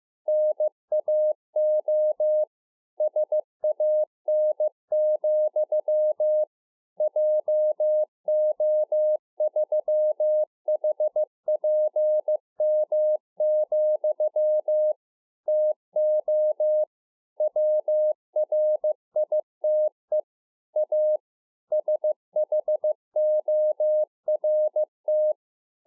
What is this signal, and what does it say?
Morse code, heard as a keyed tone, about 15 wpm, 610 Hz. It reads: NAO SAN, JO3HPM, TO WRITE A SHORT